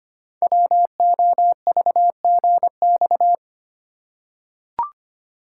Morse code, heard as a keyed tone, about 25 wpm, 690 Hz. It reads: WOVGX
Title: Code Groups - 5 Letters - 25wpm